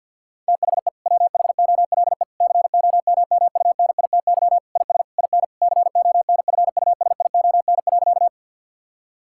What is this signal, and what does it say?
Morse code, heard as a keyed tone, about 50 wpm, 700 Hz. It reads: THE WHOLE COMMUNITY IS IN CONFUSION.
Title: Sentences from Top 1600 Words 50wpm